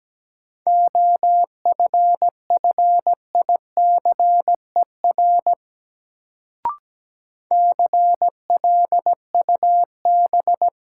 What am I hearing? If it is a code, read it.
Morse code, heard as a keyed tone, about 17 wpm, 700 Hz. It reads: OFFICER CLUB
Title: Sentences from Top 1200 Words - New Word Drill Alt - 17wpm